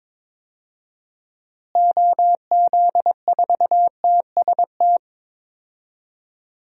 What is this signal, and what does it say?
Morse code, heard as a keyed tone, about 22 wpm, 700 Hz. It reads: OZ4TST
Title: DXCC-100 Call Signs - 6 Characters - Repeated 3x - 22wpm